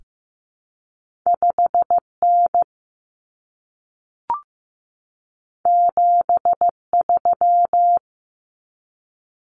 Morse code, heard as a keyed tone, about 15 wpm, 700 Hz. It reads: 5N 73
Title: Common Cut Numbers - 2-Digit 15wpm